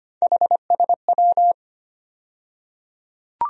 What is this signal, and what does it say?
Morse code, heard as a keyed tone, about 25 wpm, 700 Hz. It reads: HSW
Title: USA Call Sign Postfixes - 3 Letters 25wpm